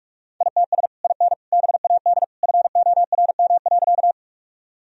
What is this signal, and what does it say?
Morse code, heard as a keyed tone, about 46 wpm, 700 Hz. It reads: ITS IN BAD FORM.
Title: Sentences from Top 1000 Words - Review Part 1 - 45wpm